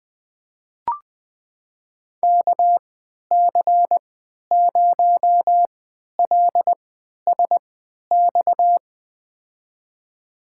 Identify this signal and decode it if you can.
Morse code, heard as a keyed tone, about 21 wpm, spaced as for 12 wpm, 700 Hz. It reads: KC0LSX
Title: USA Call Signs - 3x Character Spacing - 20wpm